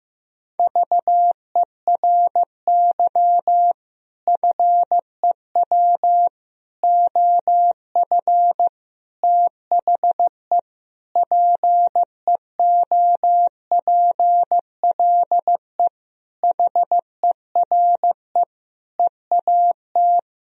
Morse code, heard as a keyed tone, about 15 wpm, 700 Hz. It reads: VERY FEW OF THE PEOPLE HERE EAT
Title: Sentences from Top 1000 Words - Review Part 3 - 15wpm